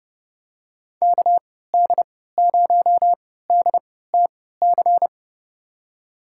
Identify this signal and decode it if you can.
Morse code, heard as a keyed tone, about 30 wpm, spaced as for 17 wpm, 700 Hz. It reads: KD0DTC
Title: USA Call Signs - 6 Characters - 3x Character Spacing - 30wpm